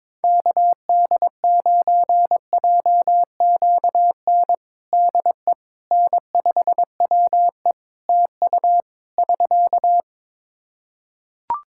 Morse code, heard as a keyed tone, about 22 wpm, 700 Hz. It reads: KD9JQN DE N5WE TU <SK>
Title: Practice QSO - Exchange 4 - Ending - 22wpm